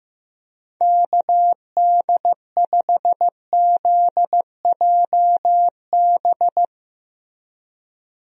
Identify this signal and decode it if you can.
Morse code, heard as a keyed tone, about 15 wpm, 700 Hz. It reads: KD5ZJB
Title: USA Call Signs - 6 Characters - Repeated 3x - 15wpm